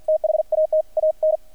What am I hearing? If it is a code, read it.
Morse code, heard as a keyed tone, about 50 wpm, 620 Hz, with radio noise. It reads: THAT AN
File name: archived / CRSnet